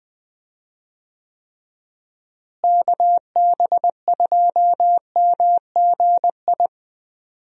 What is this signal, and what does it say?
Morse code, heard as a keyed tone, about 20 wpm, 700 Hz. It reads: KB2MGI
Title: USA Call Signs - 6 Characters - Repeated 2x - 20wpm